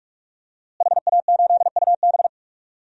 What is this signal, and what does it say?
Morse code, heard as a keyed tone, about 47 wpm, 700 Hz. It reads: HA8UB
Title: DXCC-100 Call Signs - 5 Characters - Repeated 3x - 45wpm